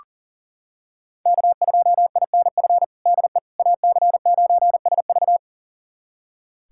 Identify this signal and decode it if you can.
Morse code, heard as a keyed tone, about 40 wpm, 700 Hz. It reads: K2INF DE AC9SV
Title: USA Callsign DE Callsign - 5 Characters - 40wpm